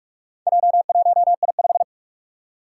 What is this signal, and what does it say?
Morse code, heard as a keyed tone, about 46 wpm, 700 Hz. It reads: J1I5
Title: Four Challenging Letter-Number Combinations 45wpm